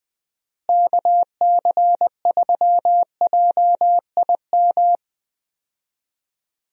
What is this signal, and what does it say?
Morse code, heard as a keyed tone, about 20 wpm, 700 Hz. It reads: KC3JIM
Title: USA Call Signs - 6 Characters - Repeated 3x - 20wpm